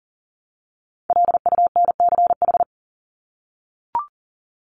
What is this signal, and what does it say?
Morse code, heard as a keyed tone, about 40 wpm, 700 Hz. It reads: LUNCH
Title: Top 1751-2000 Words 40wpm